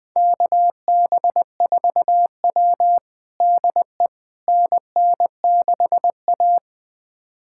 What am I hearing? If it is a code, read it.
Morse code, heard as a keyed tone, about 20 wpm, 700 Hz. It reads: KB4W DE NN6A
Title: USA Callsign DE Callsign - 4 Characters - 20wpm